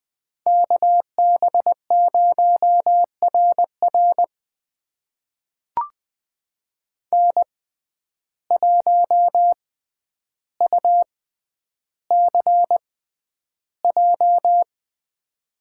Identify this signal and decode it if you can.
Morse code, heard as a keyed tone, about 20 wpm, spaced as for 10 wpm, 700 Hz. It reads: KB0RR N1UCJ
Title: USA Call Signs - 5 Characters - 6x Character Spacing - 20wpm